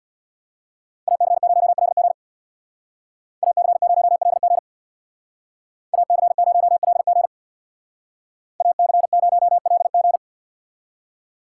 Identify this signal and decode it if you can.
Morse code, heard as a keyed tone, about 58 wpm, 700 Hz. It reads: AC0LG AC0LG AC0LG AC0LG
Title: Speed Racing - USA 2x2 Call Signs 50wpm